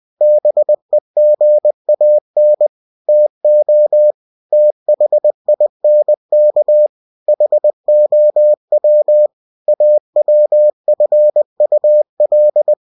Morse code, heard as a keyed tone, about 20 wpm, 600 Hz. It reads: BEGAN TO THINK HOW AWFUL
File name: cw-books / winnie-the-pooh / winnie-20wpm-page-0103